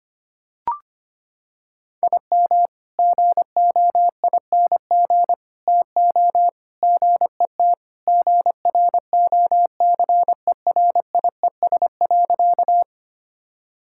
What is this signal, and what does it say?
Morse code, heard as a keyed tone, about 25 wpm, 700 Hz. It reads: IM GOING TO GET GROCERIES.
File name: practice / 25wpm / Sentences with 3-Word Phrases - 25wpm